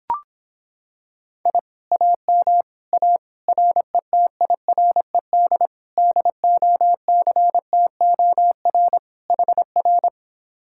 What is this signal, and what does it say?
Morse code, heard as a keyed tone, about 26 wpm, 700 Hz. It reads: I AM A RETIRED DOCTOR HR